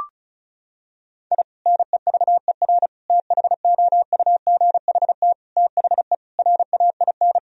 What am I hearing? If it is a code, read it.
Morse code, heard as a keyed tone, about 36 wpm, 700 Hz. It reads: I NEVER THOUGHT THE RAIN
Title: Sentences from Top 1000 Words - Review Part 1 - 35wpm